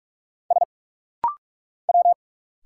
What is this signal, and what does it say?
Morse code, heard as a keyed tone, about 46 wpm, 700 Hz. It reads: S W
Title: Single Letters - Rapid-Fire 45wpm